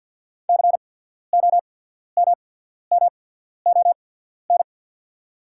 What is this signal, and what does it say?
Morse code, heard as a keyed tone, about 48 wpm, spaced as for 14 wpm, 700 Hz. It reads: COMMON
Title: 8x Character Spacing - Top 500 Words - 50wpm